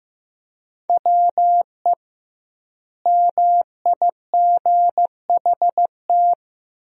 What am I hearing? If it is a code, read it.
Morse code, heard as a keyed tone, about 15 wpm, 700 Hz. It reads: WE MIGHT
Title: 2x Word-Spacing - Sentences from Top 300 Words - 15wpm